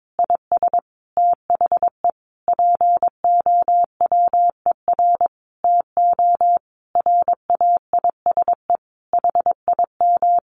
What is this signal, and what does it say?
Morse code, heard as a keyed tone, about 22 wpm, 700 Hz. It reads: IS THE POWER TO RAISE HIM
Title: Sentences from Top 1400 Words 22wpm